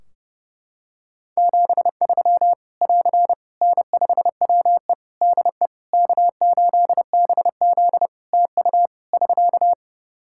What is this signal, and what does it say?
Morse code, heard as a keyed tone, about 30 wpm, 700 Hz. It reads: 73 <AR> N5WE DE K8BZ TU <SK>
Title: K1USN SST QSO Practice 30wpm